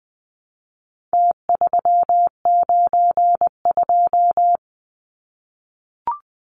Morse code, heard as a keyed tone, about 20 wpm, 700 Hz. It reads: T392
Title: Numbers - Contest Style - 4-Digit 20wpm